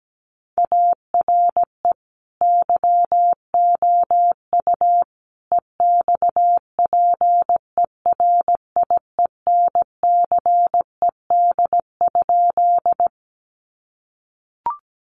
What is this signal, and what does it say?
Morse code, heard as a keyed tone, about 17 wpm, 700 Hz. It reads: ARE YOU EXPERIENCED?
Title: Sentences from Top 1100 Words 17wpm